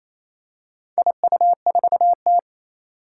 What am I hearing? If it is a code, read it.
Morse code, heard as a keyed tone, about 27 wpm, 700 Hz. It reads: IU4T
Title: Four Challenging Letter-Number Combinations 28wpm